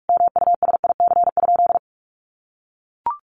Morse code, heard as a keyed tone, about 44 wpm, 700 Hz. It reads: MUSIC?
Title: Sentences from Top 300 Words 45wpm